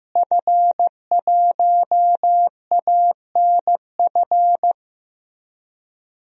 Morse code, heard as a keyed tone, about 15 wpm, 700 Hz. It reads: F1ANF
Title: DXCC-100 Call Signs - 5 Characters - Repeated 3x - 15wpm